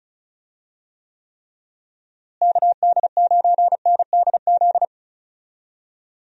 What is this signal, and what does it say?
Morse code, heard as a keyed tone, about 36 wpm, 700 Hz. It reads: KD9NDZ
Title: USA Call Signs - 6 Characters - Repeated 3x - 35wpm